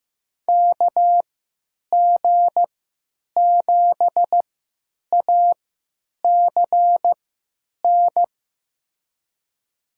Morse code, heard as a keyed tone, about 15 wpm, spaced as for 9 wpm, 700 Hz. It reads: KG7ACN
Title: USA Call Signs - 6 Characters - 3x Character Spacing - 15wpm